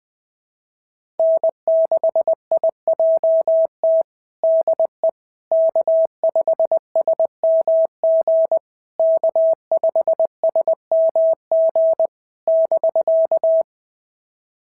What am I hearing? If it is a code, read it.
Morse code, heard as a keyed tone, about 19 wpm, 650 Hz. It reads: N6IJT DE K5SMG K5SMG <BK>
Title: SOTA QSO Practice 20wpm